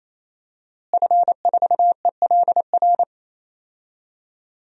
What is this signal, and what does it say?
Morse code, heard as a keyed tone, about 28 wpm, 700 Hz. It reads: F4ELR
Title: DXCC-100 Call Signs - 5 Characters - Repeated 3x - 28wpm